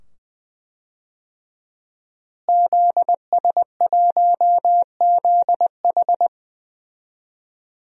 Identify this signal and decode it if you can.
Morse code, heard as a keyed tone, about 20 wpm, 700 Hz. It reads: ZS1ZH
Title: DXCC-100 Call Signs - 5 Characters - Repeated 2x - 20wpm